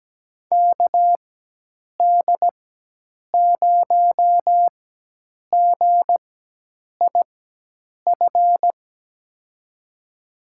Morse code, heard as a keyed tone, about 17 wpm, spaced as for 8 wpm, 700 Hz. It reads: KD0GIF
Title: USA Call Signs - 6 Characters - 4x Character Spacing - 17wpm